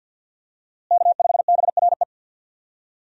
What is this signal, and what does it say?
Morse code, heard as a keyed tone, about 52 wpm, 700 Hz. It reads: K5BRE
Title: USA Call Signs - 5 Characters - Repeated 3x - 50wpm